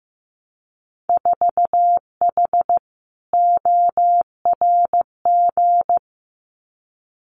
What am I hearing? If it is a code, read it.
Morse code, heard as a keyed tone, about 15 wpm, 700 Hz. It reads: 4H ORG
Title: Winter Field Day Exchanges Encore 15wpm